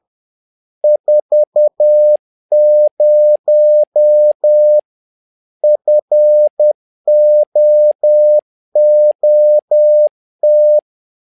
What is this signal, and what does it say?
Morse code, heard as a keyed tone, about 10 wpm, 600 Hz. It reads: 40 FOOT
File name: learning center / tower trouble / tower trouble 10wpm